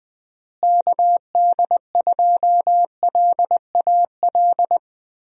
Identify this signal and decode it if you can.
Morse code, heard as a keyed tone, about 20 wpm, 700 Hz. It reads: KD2LAL
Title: USA Call Signs - 6 Characters - Repeated 2x - 20wpm